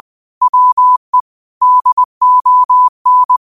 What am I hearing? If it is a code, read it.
Morse code, heard as a keyed tone, about 20 wpm, 1000 Hz. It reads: WE DON